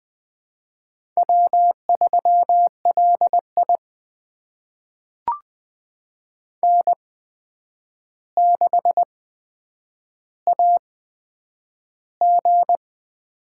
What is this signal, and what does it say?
Morse code, heard as a keyed tone, about 20 wpm, spaced as for 9 wpm, 700 Hz. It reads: W3LI N6AG
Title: USA Call Signs - 8x Character Spacing - 20wpm